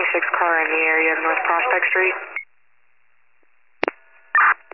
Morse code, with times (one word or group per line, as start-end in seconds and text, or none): none